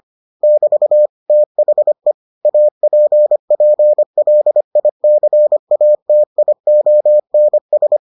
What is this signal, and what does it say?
Morse code, heard as a keyed tone, about 25 wpm, 600 Hz. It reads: <BT> THE APPLICATIONS